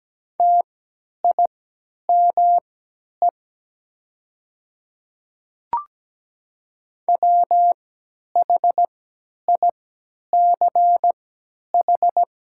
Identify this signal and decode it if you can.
Morse code, heard as a keyed tone, about 17 wpm, spaced as for 10 wpm, 700 Hz. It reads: TIME WHICH
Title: 3x Character Spacing - Top 200 Words - 17wpm